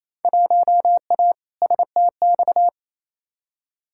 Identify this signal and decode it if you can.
Morse code, heard as a keyed tone, about 28 wpm, 700 Hz. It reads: 1A STX
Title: Field Day Exchanges Encore - 28wpm